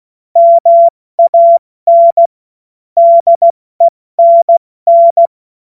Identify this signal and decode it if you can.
Morse code, heard as a keyed tone, about 16 wpm, 680 Hz. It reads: MAN DENN
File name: Morsekurs / Witze / Witz 27. Januar 2026 70 F80